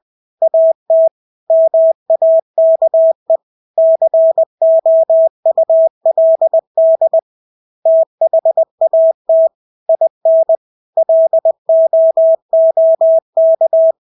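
Morse code, heard as a keyed tone, about 20 wpm, 650 Hz. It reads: AT MAKE COULD THAT IN LOOK
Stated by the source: Morse practice of English words at 20 wpm.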